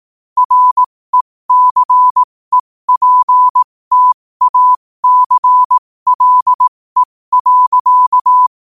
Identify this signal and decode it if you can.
Morse code, heard as a keyed tone, about 18 wpm, spaced as for 15 wpm, 1000 Hz. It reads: RECEPTACLE.
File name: qotdfiles / 15wpm